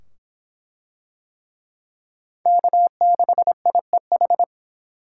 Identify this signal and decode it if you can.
Morse code, heard as a keyed tone, about 24 wpm, 700 Hz. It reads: K6IEH